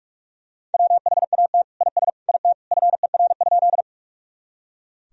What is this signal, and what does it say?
Morse code, heard as a keyed tone, about 46 wpm, 700 Hz. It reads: WHAT IS IT FER?
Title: Sentences from Top 100 Words and Abbreviations 45wpm